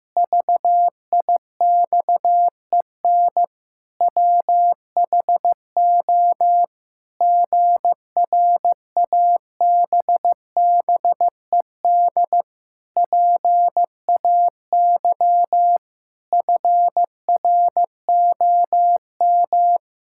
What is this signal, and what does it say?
Morse code, heard as a keyed tone, about 15 wpm, 700 Hz. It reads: VIXEN WHO GRABBED PAY FROM
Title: Pangram Challenge v2 - 15wpm